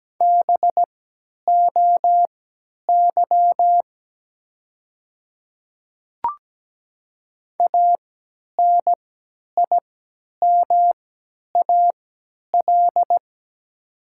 Morse code, heard as a keyed tone, about 17 wpm, spaced as for 10 wpm, 700 Hz. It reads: BOY ANIMAL
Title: 3x Character Spacing - Top 300 Words - 17wpm